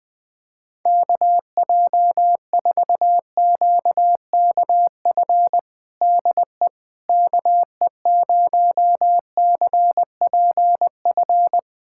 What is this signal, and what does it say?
Morse code, heard as a keyed tone, about 20 wpm, 700 Hz. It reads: KJ4QKF DE KE0CPF
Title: USA Callsign DE Callsign - 6 Characters - 20wpm